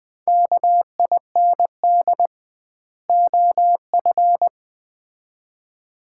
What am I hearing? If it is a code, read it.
Morse code, heard as a keyed tone, about 20 wpm, 700 Hz. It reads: KIND OF
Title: 2x Word-Spacing - Sets of 2 Words - 20wpm